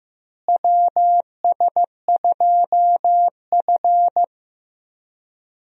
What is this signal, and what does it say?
Morse code, heard as a keyed tone, about 16 wpm, 700 Hz. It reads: WS2F
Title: USA Call Signs - 4 Characters - Repeated 3x - 15wpm